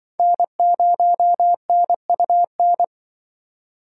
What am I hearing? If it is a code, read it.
Morse code, heard as a keyed tone, about 25 wpm, 700 Hz. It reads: N0NUN
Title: Speed Racing - USA 1x3 Call Signs 20wpm